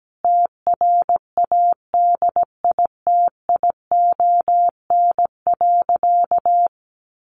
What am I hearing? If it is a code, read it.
Morse code, heard as a keyed tone, about 17 wpm, 700 Hz. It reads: TRADITION.
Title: Sentences from Top 1500 Words 17wpm